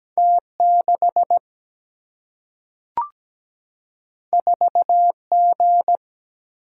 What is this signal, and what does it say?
Morse code, heard as a keyed tone, about 17 wpm, 700 Hz. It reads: T6 4G
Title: Single-Letter Single-Number 17wpm